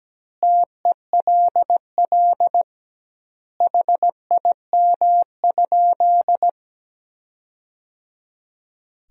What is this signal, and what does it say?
Morse code, heard as a keyed tone, about 17 wpm, 700 Hz. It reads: TELL HIM?
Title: 2x Word-Spacing - Sentences from Top 200 Words - 17wpm